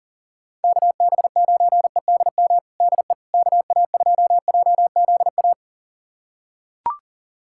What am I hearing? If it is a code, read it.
Morse code, heard as a keyed tone, about 40 wpm, 700 Hz. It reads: KB9EDM DE KA2JZA
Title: USA Callsign DE Callsign 40wpm